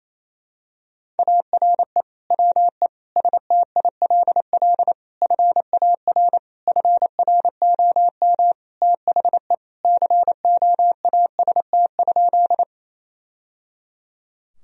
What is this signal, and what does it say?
Morse code, heard as a keyed tone, about 28 wpm, 700 Hz. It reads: ARE WE STILL FAR FROM THE COAST?
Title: Sentences from Top 1000 Words - Review Part 4 - 28wpm